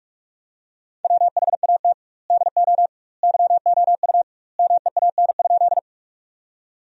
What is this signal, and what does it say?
Morse code, heard as a keyed tone, about 41 wpm, 700 Hz. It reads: WHAT DO YOU MEAN?
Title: 2x Word-Spacing - Sentences from Top 200 Words - 45wpm